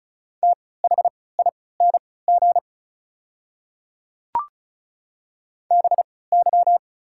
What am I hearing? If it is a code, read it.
Morse code, heard as a keyed tone, about 35 wpm, spaced as for 20 wpm, 700 Hz. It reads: THING BY
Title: 3x Character Spacing - Top 100 Words - 35wpm